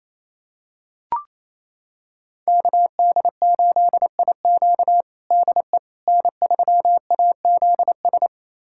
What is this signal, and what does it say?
Morse code, heard as a keyed tone, about 28 wpm, 700 Hz. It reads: KD8IQ DE N3AZS
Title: USA Callsign DE Callsign - 5 Characters - 28wpm